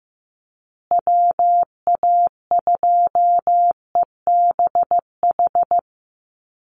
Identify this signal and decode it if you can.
Morse code, heard as a keyed tone, about 15 wpm, 700 Hz. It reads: WA2EBH
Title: USA Call Signs Repeated 2x 15wpm